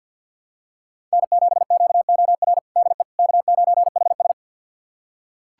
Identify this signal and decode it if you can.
Morse code, heard as a keyed tone, about 46 wpm, 700 Hz. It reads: N7QOR DE K9HS
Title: USA Callsign DE Callsign 50wpm